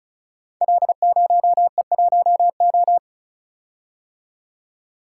Morse code, heard as a keyed tone, about 36 wpm, 700 Hz. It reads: L0E1O
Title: Code Groups - 5 Letter:Numbers - 35wpm